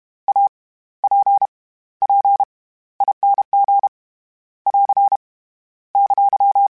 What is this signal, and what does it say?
Morse code, heard as a keyed tone, about 32 wpm, spaced as for 17 wpm, 800 Hz. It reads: APPING <AR> !
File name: Special INGs / Special INGs-mp3s / SlowToFaster ing @30